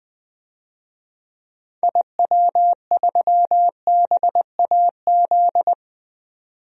Morse code, heard as a keyed tone, about 20 wpm, 700 Hz. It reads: IW3BAZ
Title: DXCC-100 Call Signs - 6 Characters - 20wpm